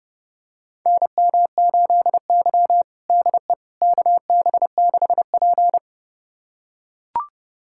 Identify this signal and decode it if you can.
Morse code, heard as a keyed tone, about 32 wpm, 700 Hz. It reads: NM8Y DE KB6P
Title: USA Callsign DE Callsign - 4 Characters - 30wpm